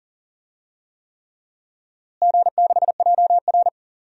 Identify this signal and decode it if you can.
Morse code, heard as a keyed tone, about 40 wpm, 700 Hz. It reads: G6JR
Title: DXCC-100 Call Signs - 4 Characters - Repeated 3x - 40wpm